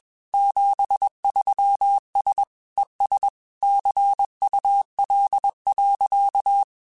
Morse code, heard as a keyed tone, about 21 wpm, 800 Hz. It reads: 73S ES CUL.
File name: QSOs / QSOs-mp3s / kc7haa@20